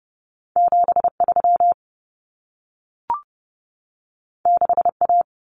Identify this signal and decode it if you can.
Morse code, heard as a keyed tone, about 30 wpm, 700 Hz. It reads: 73 6A
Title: Common Cut Numbers - 2-Digit 30wpm